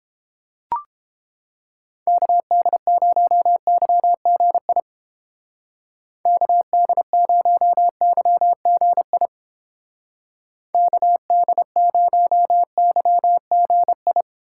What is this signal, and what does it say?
Morse code, heard as a keyed tone, about 30 wpm, 700 Hz. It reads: KD0YGI KD0YGI KD0YGI